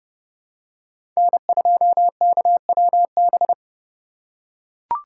Morse code, heard as a keyed tone, about 30 wpm, 700 Hz. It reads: N2KWB
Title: USA Call Signs - 5 Characters - 3x Character Spacing - 30wpm